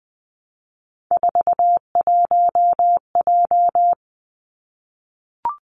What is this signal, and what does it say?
Morse code, heard as a keyed tone, about 20 wpm, 700 Hz. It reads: 41J
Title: Three Challenging Letter-Number Combinations 20wpm